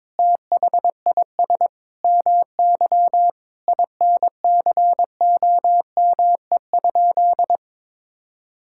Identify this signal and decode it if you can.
Morse code, heard as a keyed tone, about 22 wpm, 700 Hz. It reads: THIS MY INCOME?
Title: Sentences from Top 1000 Words - Review Part 5 - 22wpm